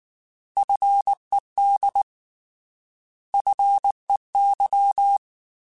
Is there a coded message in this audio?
Yes: Morse code from 0.6 to 5.2 s, heard as a keyed tone, about 19 wpm, 800 Hz. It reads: FED FEY